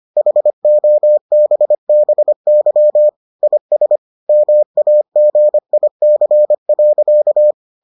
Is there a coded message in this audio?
Yes: Morse code from 0.2 to 7.5 s, heard as a keyed tone, about 25 wpm, 600 Hz. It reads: HOBBY IS MAGIC.